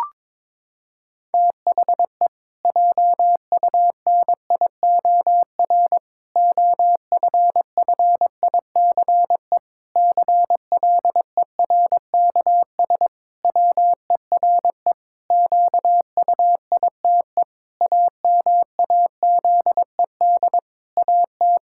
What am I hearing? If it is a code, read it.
Morse code, heard as a keyed tone, about 22 wpm, 700 Hz. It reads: THE JUNIOR OFFICE CLERKS WERE QUITE AMAZED AT
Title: Pangram Challenge v2 - 22wpm